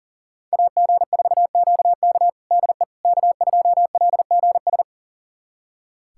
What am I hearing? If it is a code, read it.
Morse code, heard as a keyed tone, about 40 wpm, 700 Hz. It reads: AG4QK DE K2LGS
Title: USA Callsign DE Callsign - 5 Characters - 40wpm